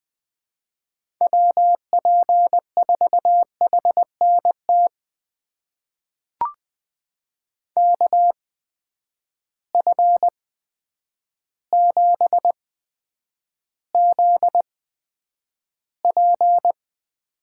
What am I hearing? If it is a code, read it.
Morse code, heard as a keyed tone, about 20 wpm, spaced as for 9 wpm, 700 Hz. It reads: WP4HNT KF7ZP
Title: USA Call Signs - 6 Characters - 8x Character Spacing - 20wpm